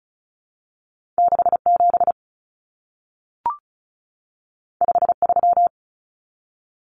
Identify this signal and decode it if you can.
Morse code, heard as a keyed tone, about 36 wpm, 700 Hz. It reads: 67 53
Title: Common Cut Numbers - 2-Digit 35wpm